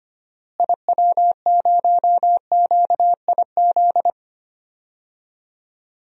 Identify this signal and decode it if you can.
Morse code, heard as a keyed tone, about 25 wpm, 700 Hz. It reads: IW0QIZ
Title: DXCC-100 Call Signs - 6 Characters - Repeated 3x - 25wpm